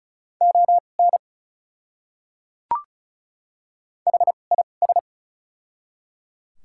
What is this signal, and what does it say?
Morse code, heard as a keyed tone, about 36 wpm, spaced as for 25 wpm, 690 Hz. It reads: ON HIS
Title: 2x Character Spacing - Top 100 Words - 35wpm